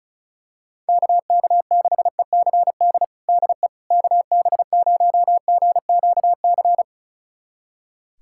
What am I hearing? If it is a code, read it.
Morse code, heard as a keyed tone, about 36 wpm, 700 Hz. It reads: KK6ECD DE KB0GQC
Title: USA Callsign DE Callsign - 6 Characters - 35wpm